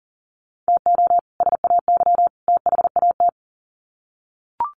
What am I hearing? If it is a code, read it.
Morse code, heard as a keyed tone, about 40 wpm, 700 Hz. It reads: TO SAY THAT
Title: Sets of 3 Words 40wpm